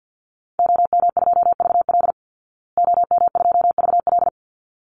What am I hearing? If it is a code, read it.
Morse code, heard as a keyed tone, about 50 wpm, 700 Hz. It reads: OM2VL OM2VL
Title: Call Signs Repeated 3x - 50wpm